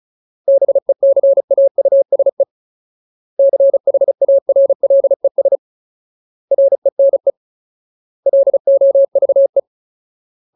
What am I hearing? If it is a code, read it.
Morse code, heard as a keyed tone, about 34 wpm, 550 Hz. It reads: BECAUSE CHARLES RENE LOVE
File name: Ultimate Word List / 35WPM Ultimate